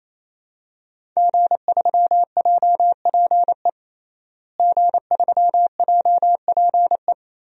Morse code, heard as a keyed tone, about 28 wpm, 700 Hz. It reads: G3JPE G3JPE
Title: DXCC-100 Call Signs - 5 Characters - Repeated 3x - 28wpm